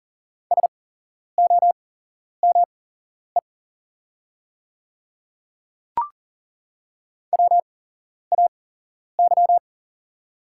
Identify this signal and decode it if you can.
Morse code, heard as a keyed tone, about 38 wpm, spaced as for 11 wpm, 700 Hz. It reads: SOME WAY
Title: 8x Character Spacing - Top 100 Words - 40wpm